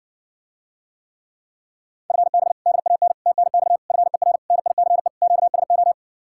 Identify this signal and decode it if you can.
Morse code, heard as a keyed tone, about 60 wpm, 700 Hz. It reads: FB DAN TNX FER NICE QSO